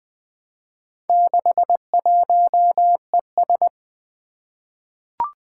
Morse code, heard as a keyed tone, about 20 wpm, 700 Hz. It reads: 61ES
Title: Four Challenging Letter-Number Combinations 20wpm